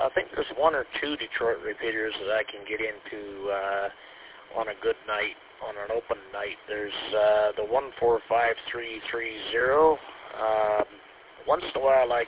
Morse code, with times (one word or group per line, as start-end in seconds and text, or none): none